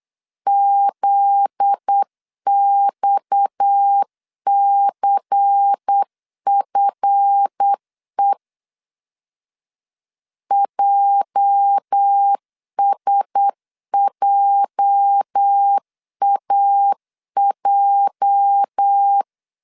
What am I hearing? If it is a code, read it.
Morse code, heard as a keyed tone, about 8 wpm, 800 Hz. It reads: ZXCFE JSJAJ
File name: LEZIONI / 8 YJ45 / 8 YJ 8wpm 3313